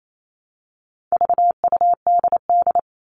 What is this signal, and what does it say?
Morse code, heard as a keyed tone, about 28 wpm, 700 Hz. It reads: VUDD
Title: All Cut Numbers - 4-Digit 28wpm